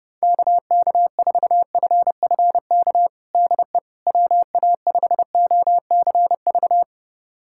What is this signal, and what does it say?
Morse code, heard as a keyed tone, about 30 wpm, 700 Hz. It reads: KK4FFK DE WA5OCV
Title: USA Callsign DE Callsign - 6 Characters - 30wpm